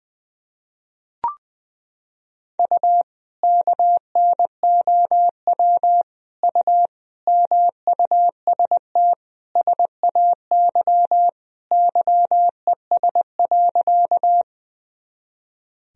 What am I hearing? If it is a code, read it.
Morse code, heard as a keyed tone, about 20 wpm, 700 Hz. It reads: U KNOW U MUST SAY YES.
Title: Sentences from Top 500 Words and Abbreviations 20wpm